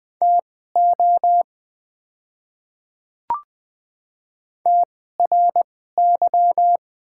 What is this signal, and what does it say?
Morse code, heard as a keyed tone, about 20 wpm, spaced as for 14 wpm, 700 Hz. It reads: TO TRY
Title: 2x Character Spacing - Top 200 Words - 20wpm